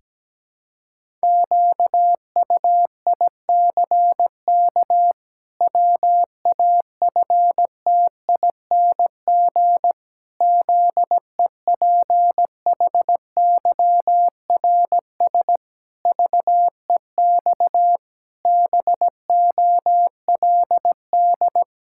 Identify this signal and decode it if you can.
Morse code, heard as a keyed tone, about 17 wpm, 700 Hz. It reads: QUICK WAFTING ZEPHYRS VEX BOLD
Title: Pangram Challenge v2 - 17wpm